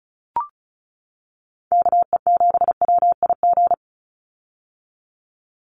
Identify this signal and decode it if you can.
Morse code, heard as a keyed tone, about 34 wpm, 700 Hz. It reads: KE7WIG
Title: USA Call Signs 35wpm